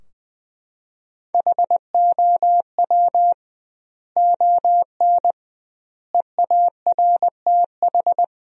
Morse code, heard as a keyed tone, about 21 wpm, 700 Hz. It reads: HOW ON EARTH
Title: 2x Word-Spacing - Sentences from Top 200 Words - 20wpm